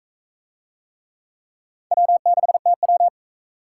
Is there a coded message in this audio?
Yes: Morse code from 1.9 to 3.1 s, heard as a keyed tone, about 44 wpm, 700 Hz. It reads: W6TW